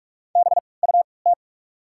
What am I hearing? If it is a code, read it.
Morse code, heard as a keyed tone, about 45 wpm, spaced as for 25 wpm, 700 Hz. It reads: BUT
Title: 3x Character Spacing - Top 200 Words - 45wpm